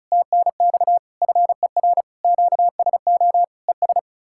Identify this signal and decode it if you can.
Morse code, heard as a keyed tone, about 46 wpm, 700 Hz. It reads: TNX FER QSO ES